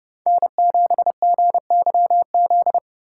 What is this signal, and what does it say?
Morse code, heard as a keyed tone, about 30 wpm, 700 Hz. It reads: N7GYZ